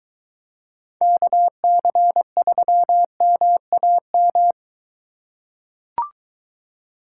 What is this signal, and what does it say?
Morse code, heard as a keyed tone, about 22 wpm, 700 Hz. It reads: KC3MAM